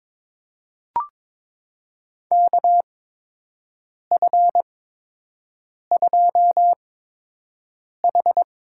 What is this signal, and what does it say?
Morse code, heard as a keyed tone, about 23 wpm, spaced as for 6 wpm, 700 Hz. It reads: KF2H
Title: USA Call Signs - 4 Characters - 8x Character Spacing - 22wpm